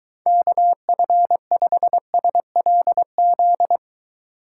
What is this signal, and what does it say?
Morse code, heard as a keyed tone, about 20 wpm, 700 Hz. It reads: KF5SLZ